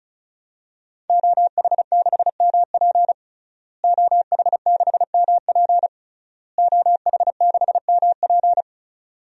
Morse code, heard as a keyed tone, about 36 wpm, 700 Hz. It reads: OH6MP OH6MP OH6MP
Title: DXCC-100 Call Signs - 5 Characters - Repeated 3x - 35wpm